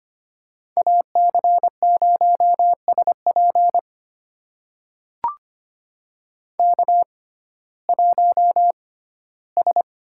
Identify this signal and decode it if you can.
Morse code, heard as a keyed tone, about 25 wpm, spaced as for 15 wpm, 700 Hz. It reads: AC0SP K1S
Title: USA Call Signs - 6x Character Spacing - 25wpm